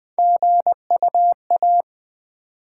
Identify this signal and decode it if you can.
Morse code, heard as a keyed tone, about 20 wpm, 700 Hz. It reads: GUA
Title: Build A Word - 3 Letters At A Time - 9-Letter Words - 20wpm